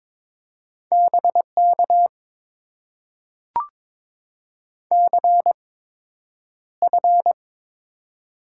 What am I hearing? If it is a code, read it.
Morse code, heard as a keyed tone, about 22 wpm, spaced as for 9 wpm, 700 Hz. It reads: BK CF
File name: Top 90 QSO Elements - 8x Character Spacing - 22wpm